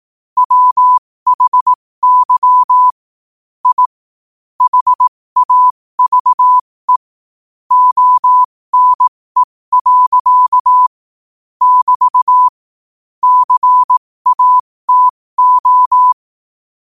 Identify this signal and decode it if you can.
Morse code, heard as a keyed tone, about 18 wpm, spaced as for 15 wpm, 1000 Hz. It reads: WHY I HAVE ONE. <BT> CATO
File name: qotdfiles / 15wpm